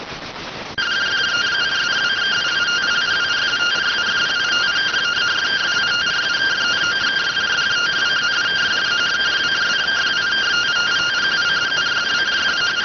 Two-tone teleprinter signal, about 170 Hz shift.